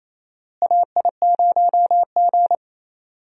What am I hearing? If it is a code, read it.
Morse code, heard as a keyed tone, about 28 wpm, 700 Hz. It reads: AI0G
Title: USA Call Signs - 4 Characters - Repeated 3x - 28wpm